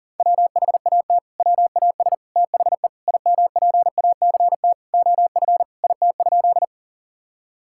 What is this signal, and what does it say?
Morse code, heard as a keyed tone, about 37 wpm, 690 Hz. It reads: WHAT WAS THE IMPACT OF IT?
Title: Sentences from Top 1000 Words - Review Part 6 - 40wpm